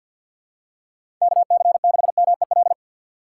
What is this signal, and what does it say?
Morse code, heard as a keyed tone, about 54 wpm, 700 Hz. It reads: KC6MEL
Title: USA Call Signs - 6 Characters - 8x Character Spacing - 50wpm